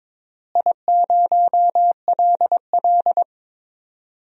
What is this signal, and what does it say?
Morse code, heard as a keyed tone, about 22 wpm, 700 Hz. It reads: I0LL
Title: DXCC-100 Call Signs - 4 Characters - Repeated 2x - 22wpm